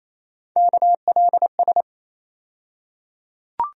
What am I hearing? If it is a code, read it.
Morse code, heard as a keyed tone, about 28 wpm, 700 Hz. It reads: KLS
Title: USA Call Sign Postfixes - 3 Letters 28wpm